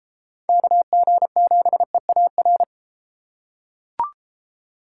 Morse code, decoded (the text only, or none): KG7EAR